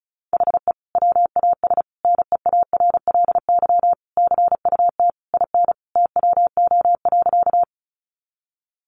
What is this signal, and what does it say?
Morse code, heard as a keyed tone, about 34 wpm, 700 Hz. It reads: HE WAS NEARLY CUT IN TWO.
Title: Sentences from Top 1300 Words 35wpm